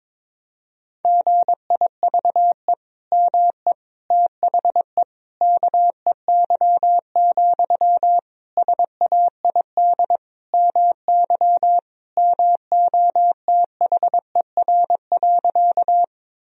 Morse code, heard as a keyed tone, about 22 wpm, 700 Hz. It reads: GIVE ME THE KEY, SAID MY MOTHER.
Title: Sentences from Top 1000 Words - Review Part 4 - 22wpm